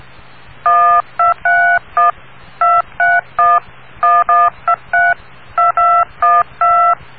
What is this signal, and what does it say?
DTMF, heard as telephone touch tones, tones of about 210 ms. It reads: 1231 231 1123 2212